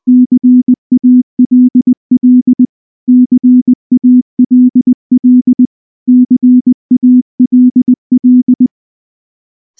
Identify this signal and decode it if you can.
Morse code, heard as a keyed tone, about 19 wpm, 250 Hz. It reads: CALL CALL CALL
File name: morse-training-20WPM